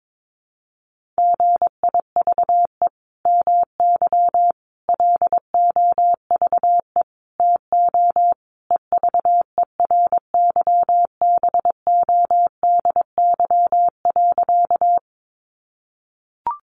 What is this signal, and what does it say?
Morse code, heard as a keyed tone, about 22 wpm, 700 Hz. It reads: GIVE MY LOVE TO EVERYBODY.
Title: Sentences from Top 1400 Words 22wpm